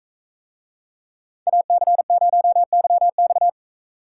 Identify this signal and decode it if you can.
Morse code, heard as a keyed tone, about 44 wpm, 700 Hz. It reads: AC0YX